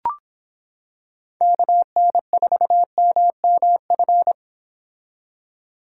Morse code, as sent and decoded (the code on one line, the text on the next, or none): -.- -. ....- -- -- ..-.
KN4MMF